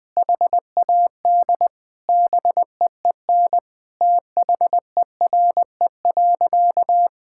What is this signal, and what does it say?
Morse code, heard as a keyed tone, about 20 wpm, 700 Hz. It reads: HAD BEEN THERE.
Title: Sentences with 4-Word Phrases - 20wpm